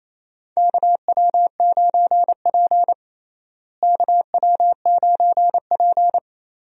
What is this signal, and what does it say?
Morse code, heard as a keyed tone, about 28 wpm, 700 Hz. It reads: KW9P KW9P
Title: USA Call Signs - 4 Characters - Repeated 3x - 28wpm